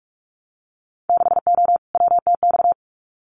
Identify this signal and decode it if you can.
Morse code, heard as a keyed tone, about 46 wpm, 700 Hz. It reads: 6O WTX
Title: Winter Field Day Exchanges Encore 45wpm